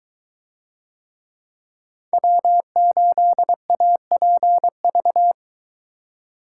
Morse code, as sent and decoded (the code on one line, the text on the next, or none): .-- ---.. .- .--. ...-
W8APV